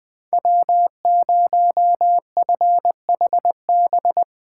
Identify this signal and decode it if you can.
Morse code, heard as a keyed tone, about 20 wpm, 700 Hz. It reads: W0FHB
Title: USA Call Signs - 5 Characters - 8x Character Spacing - 20wpm